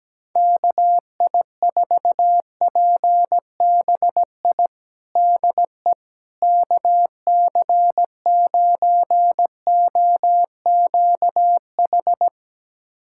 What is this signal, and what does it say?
Morse code, heard as a keyed tone, about 17 wpm, 700 Hz. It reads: KI4PBI DE KC9OQH
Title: USA Callsign DE Callsign 17wpm